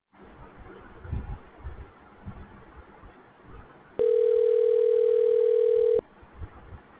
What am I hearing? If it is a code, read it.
Ringback tone (North America), 440 + 480 Hz.